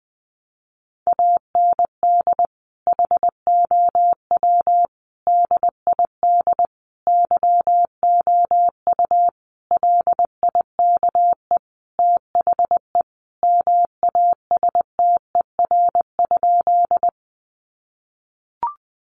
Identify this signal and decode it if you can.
Morse code, heard as a keyed tone, about 20 wpm, 700 Hz. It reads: AND HOW DID YOU LIKE THE MASTER?
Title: Sentences from Top 1300 Words 20wpm